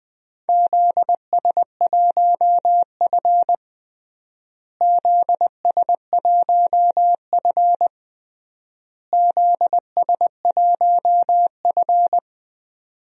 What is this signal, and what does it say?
Morse code, heard as a keyed tone, about 20 wpm, 700 Hz. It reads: ZS1F ZS1F ZS1F
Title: DXCC-100 Call Signs - 4 Characters - Repeated 3x - 20wpm